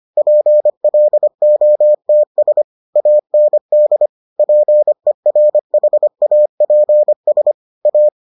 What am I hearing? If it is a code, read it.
Morse code, heard as a keyed tone, about 25 wpm, 600 Hz. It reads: PLOTS AND PERHAPS A